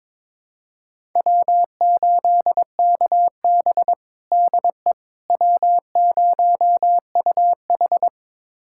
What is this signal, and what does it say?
Morse code, heard as a keyed tone, about 22 wpm, 700 Hz. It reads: W8KB DE W0UH
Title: USA Callsign DE Callsign - 4 Characters - 22wpm